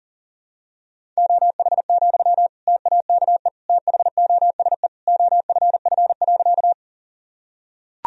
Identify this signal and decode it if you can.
Morse code, heard as a keyed tone, about 40 wpm, 700 Hz. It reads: OH, TAKE THOSE OFF.
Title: Sentences from Top 1000 Words - Review Part 1 - 40wpm